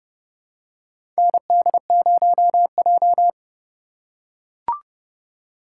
Morse code, heard as a keyed tone, about 31 wpm, 700 Hz. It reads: ND0J
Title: USA Call Signs - 4x Character Spacing - 30wpm